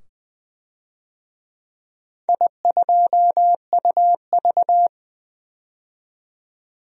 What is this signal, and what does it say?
Morse code, heard as a keyed tone, about 21 wpm, 700 Hz. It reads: I2UV
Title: DXCC-100 Call Signs - 4 Characters - 20wpm